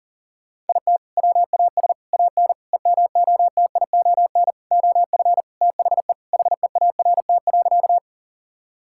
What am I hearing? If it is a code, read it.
Morse code, heard as a keyed tone, about 40 wpm, 700 Hz. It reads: IT WAS AN EMOTION OF THE HEART.
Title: Sentences from Top 1600 Words 40wpm